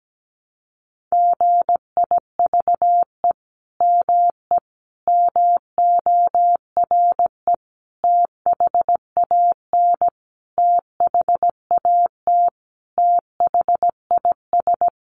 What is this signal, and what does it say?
Morse code, heard as a keyed tone, about 17 wpm, 700 Hz. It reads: GIVE ME MORE THAN THAT THIS
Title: Sentences from Top 200 Words 17wpm